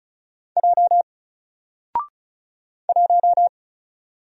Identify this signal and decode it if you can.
Morse code, heard as a keyed tone, about 36 wpm, 700 Hz. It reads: J 1
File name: Single-Letter-Number-35wpm